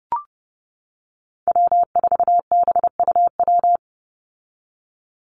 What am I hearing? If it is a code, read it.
Morse code, heard as a keyed tone, about 30 wpm, 700 Hz. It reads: W4BUW
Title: Call Signs - 30wpm